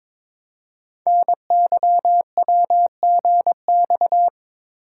Morse code, heard as a keyed tone, about 22 wpm, 700 Hz. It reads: NYWGX
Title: Code Groups - 5 Letter:Numbers - 22wpm